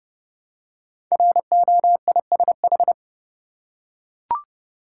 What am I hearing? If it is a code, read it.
Morse code, heard as a keyed tone, about 29 wpm, 700 Hz. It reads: ROISH
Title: Code Groups - 5 Letters - 30wpm